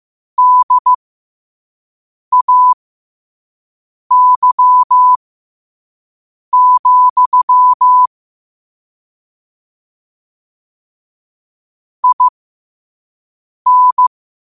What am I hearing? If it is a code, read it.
Morse code, heard as a keyed tone, about 15 wpm, spaced as for 5 wpm, 1000 Hz. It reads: DAY, IN
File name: qotdfiles / 5wpm